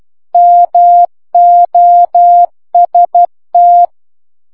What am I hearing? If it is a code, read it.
Morse code, heard as a keyed tone, about 12 wpm, 700 Hz. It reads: MOST